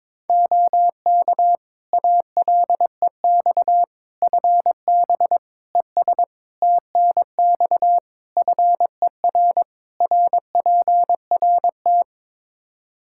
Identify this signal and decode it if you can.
Morse code, heard as a keyed tone, about 22 wpm, 700 Hz. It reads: OK ALEX FB ES TNX FER RPRT
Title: Practice QSO - Exchange 2 - Info - 22wpm